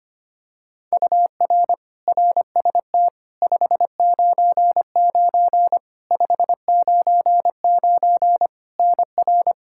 Morse code, heard as a keyed tone, about 25 wpm, 700 Hz. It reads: UR RST 599 599 NR
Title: Practice QSO - Exchange 1 - Beginning - 25wpm